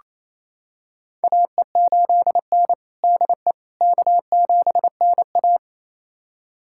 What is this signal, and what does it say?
Morse code, heard as a keyed tone, about 27 wpm, 690 Hz. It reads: AE8N DE K7NA
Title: USA Callsign DE Callsign - 4 Characters - 28wpm